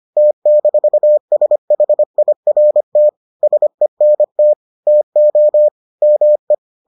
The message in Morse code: T-SHIRT SENT TO ME

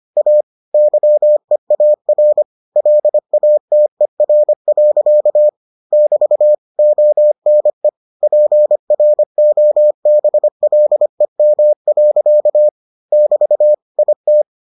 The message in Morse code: A YEAR LATER. <BT> ONE PROBLEM. <BT> IT

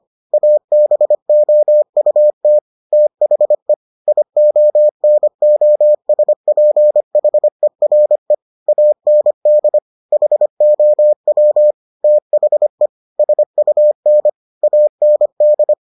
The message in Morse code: ABOUT THE IONOSPHERE AND HOW THE SUN AND

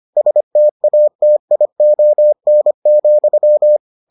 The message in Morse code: STATION,

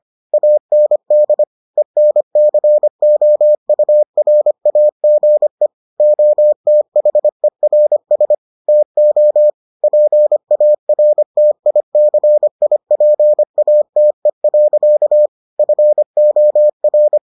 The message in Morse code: AND ENCOURAGE OTHERS TO PARTICIPATE. FOR

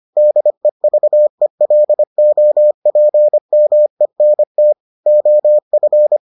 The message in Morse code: DEVELOPMENT OF

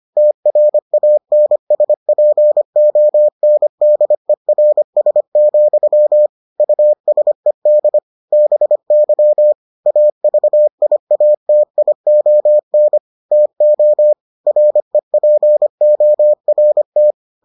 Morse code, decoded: TRANSPONDERS, USED BY AVIATION TO REPORT

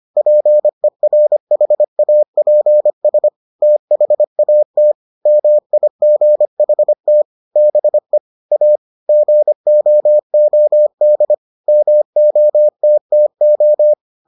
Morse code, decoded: PERHAPS THAT MIGHT BE A GOOD MOTTO

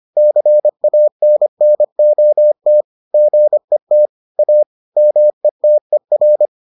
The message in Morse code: CANNOT GET A METER

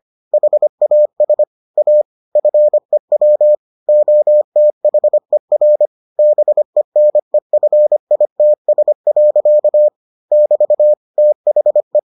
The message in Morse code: HAS A FEW OTHER BENEFITS. <BT> THE

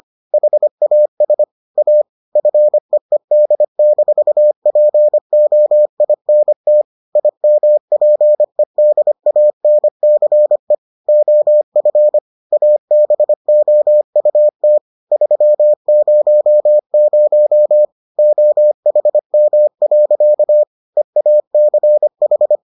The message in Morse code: HAS A FEED-POINT IMPEDANCE OF ABOUT 300 OHM. EACH

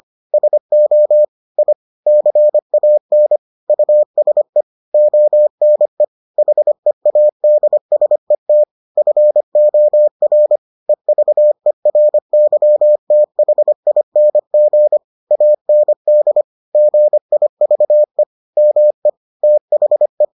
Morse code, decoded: SO I CAN USE ONE HEADSET FOR EVERYTHING AND GIVE ME THE

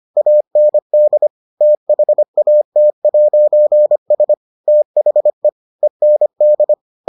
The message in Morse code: AND THAT'S THE END

